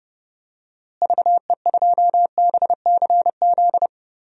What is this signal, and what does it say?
Morse code, heard as a keyed tone, about 30 wpm, 700 Hz. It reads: VE2BCZ
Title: DXCC-100 Call Signs - 6 Characters - Repeated 2x - 30wpm